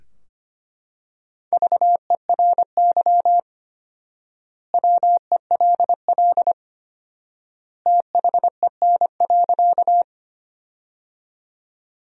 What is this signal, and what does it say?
Morse code, heard as a keyed tone, about 25 wpm, 700 Hz. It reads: VERY WELL THEN.
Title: 4x Word-Spacing - Sentences from Top 200 Words - 25wpm